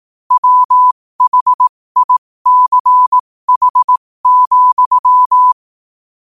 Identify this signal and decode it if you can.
Morse code, heard as a keyed tone, about 18 wpm, spaced as for 15 wpm, 1000 Hz. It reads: WHICH,
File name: qotdfiles / 15wpm